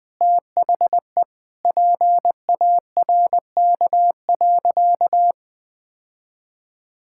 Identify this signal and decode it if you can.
Morse code, heard as a keyed tone, about 20 wpm, 700 Hz. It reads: THE PARK.
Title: Sentences with 2-Word Phrases - 20wpm